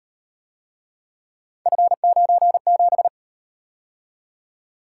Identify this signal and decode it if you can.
Morse code, heard as a keyed tone, about 37 wpm, 700 Hz. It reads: F97